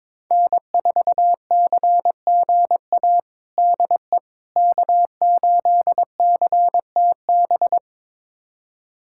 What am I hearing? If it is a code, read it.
Morse code, heard as a keyed tone, about 21 wpm, 700 Hz. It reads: N4CGA DE K8CTB
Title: USA Callsign DE Callsign - 5 Characters - 22wpm